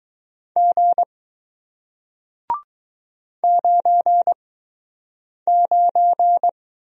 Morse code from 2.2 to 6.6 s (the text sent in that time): 99